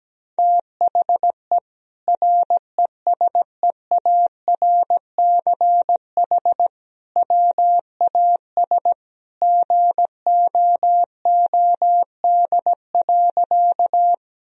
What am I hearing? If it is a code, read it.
Morse code, heard as a keyed tone, about 17 wpm, 700 Hz. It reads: THE RESEARCH WAS GOOD.
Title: Sentences from Top 1000 Words - Review Part 4 - 17wpm